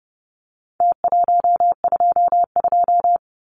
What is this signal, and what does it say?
Morse code, heard as a keyed tone, about 30 wpm, 700 Hz. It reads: T122
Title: Numbers - Contest Style - 4-Digit 30wpm